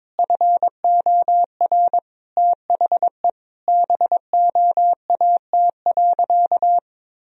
Morse code, heard as a keyed tone, about 22 wpm, 700 Hz. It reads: FOR THE BOAT.
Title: Sentences from Top 1000 Words - Review Part 3 - 22wpm